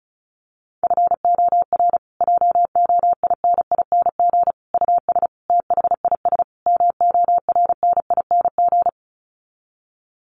Morse code, heard as a keyed tone, about 34 wpm, 700 Hz. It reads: FOR JOINING US THIS MORNING
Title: Sets of 5 Words Encore 35wpm